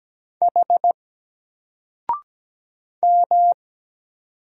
Morse code, heard as a keyed tone, about 17 wpm, 700 Hz. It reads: H M
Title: Single Letter-Number Alt 17wpm